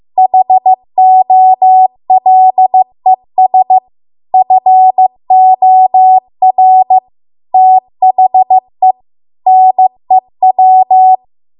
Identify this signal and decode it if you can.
Morse code, heard as a keyed tone, about 15 wpm, 750 Hz. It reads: HOLES FOR THE NEW
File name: Archive / 15 WPM / 15WPM